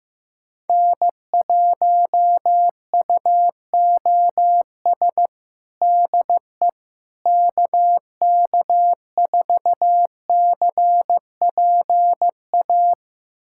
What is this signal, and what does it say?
Morse code, heard as a keyed tone, about 16 wpm, 700 Hz. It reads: N1UOS DE KK4CPA
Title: USA Callsign DE Callsign 15wpm